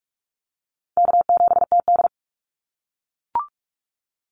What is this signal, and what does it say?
Morse code, heard as a keyed tone, about 44 wpm, 700 Hz. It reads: K7TD
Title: Call Signs - 45wpm